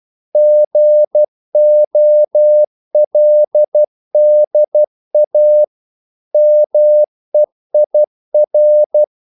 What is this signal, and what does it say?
Morse code, heard as a keyed tone, about 12 wpm, 600 Hz. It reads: GOLDA MEIR
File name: CW / Practice / EnglishQuotations0112